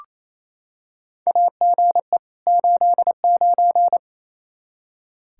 Morse code, heard as a keyed tone, about 28 wpm, 700 Hz. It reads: AGE 89
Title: Practice QSO - Exchange 3 - More Info - 28wpm